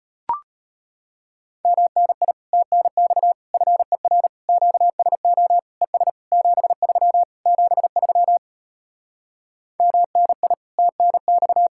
Morse code, heard as a keyed tone, about 33 wpm, 700 Hz. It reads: MNI TNX FER QSO ES 73 73 MNI TNX